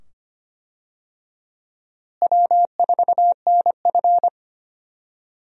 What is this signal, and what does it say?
Morse code, heard as a keyed tone, about 25 wpm, 700 Hz. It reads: W4NF
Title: USA Call Signs - 4 Characters - Repeated 2x - 25wpm